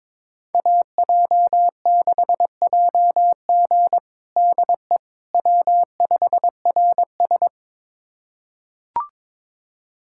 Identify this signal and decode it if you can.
Morse code, heard as a keyed tone, about 22 wpm, 700 Hz. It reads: AJ6JG DE W5RS
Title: USA Callsign DE Callsign 22wpm